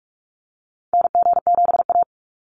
Morse code, heard as a keyed tone, about 44 wpm, 700 Hz. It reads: NG7A
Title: Call Signs Repeated 3x - 45wpm